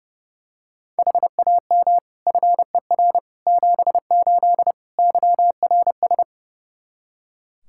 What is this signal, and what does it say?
Morse code, heard as a keyed tone, about 30 wpm, 700 Hz. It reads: HAM FER 78 YRS
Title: Practice QSO - Exchange 3 - More Info - 30wpm